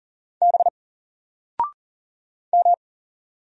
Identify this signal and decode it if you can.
Morse code, heard as a keyed tone, about 40 wpm, 700 Hz. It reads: B M
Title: Single Letter-Number Alt 40wpm